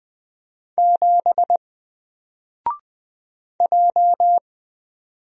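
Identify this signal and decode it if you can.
Morse code, heard as a keyed tone, about 20 wpm, 700 Hz. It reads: 7 J
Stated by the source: Single letter and number Morse drill at 20 wpm, from N4LCD.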